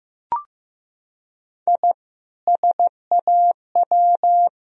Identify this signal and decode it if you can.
Morse code, heard as a keyed tone, about 16 wpm, 700 Hz. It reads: I SAW
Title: Sentences from Top 1000 Words - Review Part 1 - 15wpm